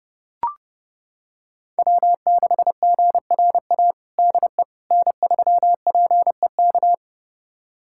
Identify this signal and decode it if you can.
Morse code, heard as a keyed tone, about 32 wpm, 700 Hz. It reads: W6GRA DE N3PEK
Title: USA Callsign DE Callsign 30wpm